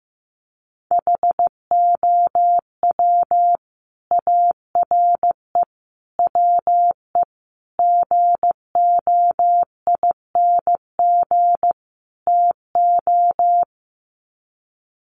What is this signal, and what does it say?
Morse code, heard as a keyed tone, about 15 wpm, 700 Hz. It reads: HOW ARE WE GOING TO
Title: Sets of 5 Words Encore 15wpm